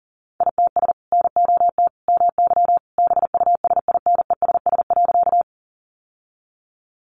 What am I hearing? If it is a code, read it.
Morse code, heard as a keyed tone, about 40 wpm, 700 Hz. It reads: ITS NOT MY BUSINESS.
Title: Sentences from Top 600 Words 40wpm